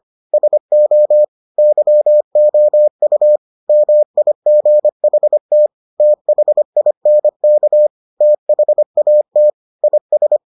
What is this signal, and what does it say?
Morse code, heard as a keyed tone, about 25 wpm, 600 Hz. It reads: SO YOU MIGHT THINK THAT IS